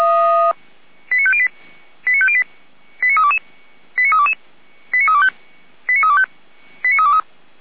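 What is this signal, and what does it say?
Five-tone selective call (ZVEI), heard as quick runs of five tones, tones of about 70 ms. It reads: 89508 89508 89322 89322 89325 89325 89323
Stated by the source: Five-tone ZVEI selective call received on an SDR.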